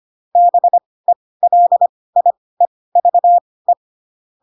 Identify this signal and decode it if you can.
Morse code, heard as a keyed tone, about 25 wpm, spaced as for 18 wpm, 700 Hz. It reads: BELIEVE
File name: Pinocchio / Pinocchio ch7 12 25 ik1hge